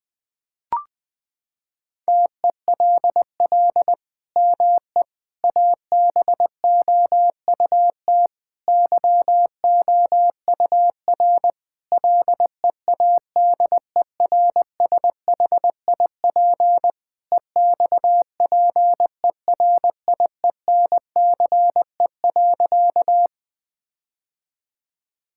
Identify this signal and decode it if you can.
Morse code, heard as a keyed tone, about 19 wpm, 700 Hz. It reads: TELL ME ABOUT YOUR LEADERSHIP EXPERIENCE.
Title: Sentences from Top 1600 Words 20wpm